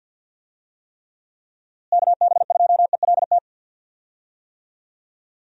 Morse code, heard as a keyed tone, about 48 wpm, 700 Hz. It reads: KB2ELT
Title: USA Call Signs - 6 Characters - Repeated 2x - 50wpm